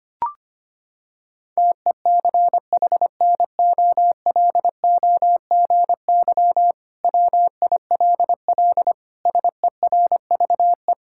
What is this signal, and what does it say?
Morse code, heard as a keyed tone, about 25 wpm, 700 Hz. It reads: TECHNOLOGY WILL SERVE
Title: Sentences from Top 1000 Words - Review Part 5 - 25wpm